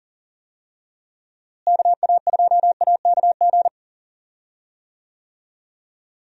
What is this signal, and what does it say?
Morse code, heard as a keyed tone, about 40 wpm, 700 Hz. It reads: KA2AKG
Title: USA Call Signs - 6 Characters - Repeated 3x - 40wpm